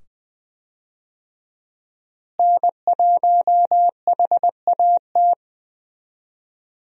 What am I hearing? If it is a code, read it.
Morse code, heard as a keyed tone, about 20 wpm, 700 Hz. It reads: N1HAT
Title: USA Call Signs - 5 Characters - Repeated 2x - 20wpm